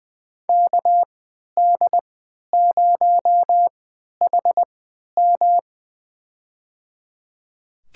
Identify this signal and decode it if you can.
Morse code, heard as a keyed tone, about 20 wpm, spaced as for 11 wpm, 700 Hz. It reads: KD0HM
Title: USA Call Signs - 3x Character Spacing - 20wpm